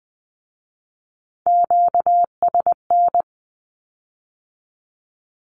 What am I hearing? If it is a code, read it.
Morse code, heard as a keyed tone, about 20 wpm, 700 Hz. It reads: QSN
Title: Q-Codes with Spoken Meaning - 20wpm